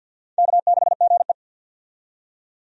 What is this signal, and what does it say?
Morse code, heard as a keyed tone, about 52 wpm, 700 Hz. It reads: K6GE
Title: USA Call Signs - 4 Characters - 50wpm